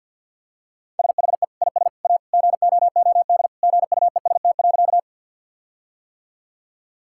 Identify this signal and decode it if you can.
Morse code, heard as a keyed tone, about 50 wpm, 700 Hz. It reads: SHE IS A GOOD GUEST.
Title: Sentences from Top 1600 Words 50wpm